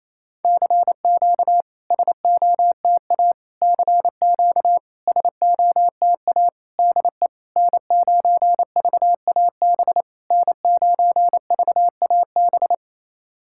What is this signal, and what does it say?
Morse code, heard as a keyed tone, about 28 wpm, 700 Hz. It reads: CQ SOTA CQ SOTA DE N9VAB N9VAB
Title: SOTA QSO Practice 28wpm